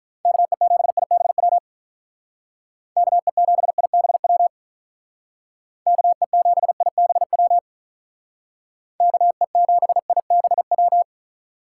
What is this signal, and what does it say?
Morse code, heard as a keyed tone, about 44 wpm, 700 Hz. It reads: KE7IBW KE7IBW KE7IBW KE7IBW